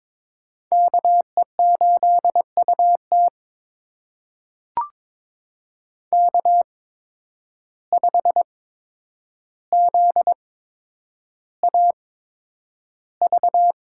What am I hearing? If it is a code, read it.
Morse code, heard as a keyed tone, about 22 wpm, spaced as for 9 wpm, 700 Hz. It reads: KE8UT K5ZAV
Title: USA Call Signs - 5 Characters - 8x Character Spacing - 22wpm